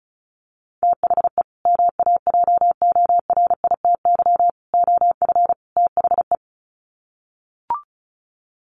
Morse code, heard as a keyed tone, about 34 wpm, 700 Hz. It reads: THE MAJORITY OF THE
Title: Sets of 4 Words 35wpm